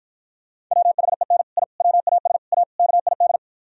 Morse code, heard as a keyed tone, about 52 wpm, 700 Hz. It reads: WHEN I WAS A KID